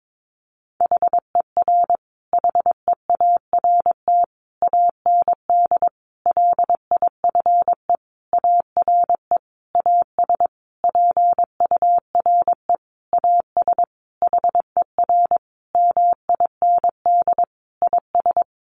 Morse code, heard as a keyed tone, about 22 wpm, 700 Hz. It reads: HER HEART AND LIFE ARE AS PURE AS HER MIND IS
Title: Sentences from Top 1500 Words 22wpm